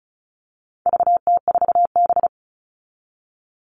Morse code, heard as a keyed tone, about 34 wpm, 700 Hz. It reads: VT4B
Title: All Cut Numbers - 4-Digit 35wpm